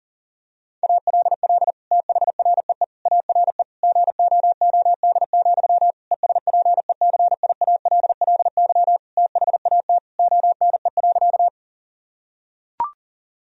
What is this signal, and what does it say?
Morse code, heard as a keyed tone, about 40 wpm, 690 Hz. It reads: ALL THREE ARE GOOD, ESPECIALLY THAT ONE.
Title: Sentences from Top 1000 Words - Review Part 4 - 40wpm